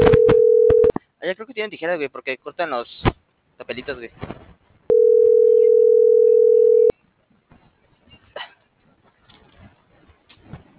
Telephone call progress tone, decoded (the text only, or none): ringback tone